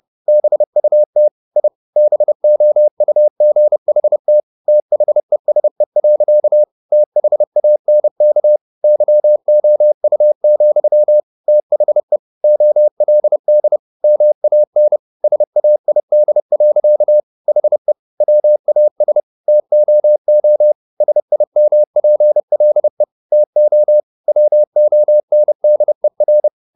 Morse code, heard as a keyed tone, about 30 wpm, 600 Hz. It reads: BUT I BOUGHT THESE. THANK YOU, THE OLD MAN SAID. HE WAS TOO SIMPLE TO WONDER
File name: the old man and the sea / the old man and the sea 1 30wpm